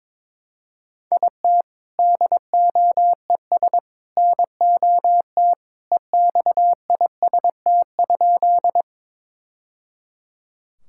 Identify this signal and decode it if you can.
Morse code, heard as a keyed tone, about 22 wpm, 700 Hz. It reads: IT DOES NOT EXIST?
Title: Sentences from Top 1000 Words - Review Part 5 - 22wpm